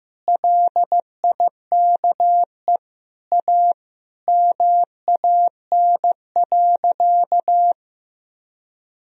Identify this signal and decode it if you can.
Morse code, heard as a keyed tone, about 15 wpm, 700 Hz. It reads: LIKE A MAN.
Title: Sentences from Top 300 Words and Abbreviations 15wpm